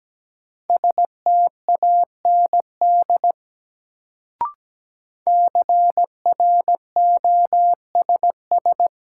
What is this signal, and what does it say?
Morse code, heard as a keyed tone, about 17 wpm, 700 Hz. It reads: STAND CROSS
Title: Sentences from Top 300 Words - New Word Drill - 17wpm